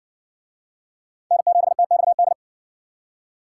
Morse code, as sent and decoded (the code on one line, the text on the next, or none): -. --... - -..- -..
N7TXD